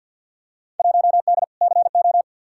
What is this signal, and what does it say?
Morse code, heard as a keyed tone, about 51 wpm, 700 Hz. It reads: 1D CO
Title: Field Day Exchanges - 50wpm